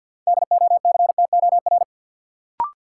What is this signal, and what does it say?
Morse code, heard as a keyed tone, about 49 wpm, 700 Hz. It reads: DOCTOR
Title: Sentences from Top 1500 Words - New Word Drill Alt - 50wpm